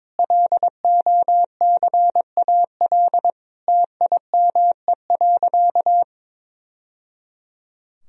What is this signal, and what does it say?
Morse code, heard as a keyed tone, about 22 wpm, 700 Hz. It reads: LOCAL TIME.
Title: Sentences from Top 1000 Words - Review Part 3 - 22wpm